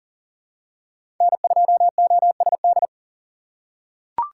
Morse code, heard as a keyed tone, about 40 wpm, 700 Hz. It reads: N2OSD
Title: USA Call Signs - 5 Characters - 3x Character Spacing - 40wpm